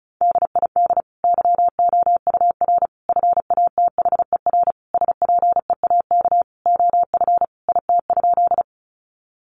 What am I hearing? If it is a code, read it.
Morse code, heard as a keyed tone, about 34 wpm, 700 Hz. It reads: DID YOUR FATHER SPEAK OF IT?
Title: Sentences from Top 1100 Words 35wpm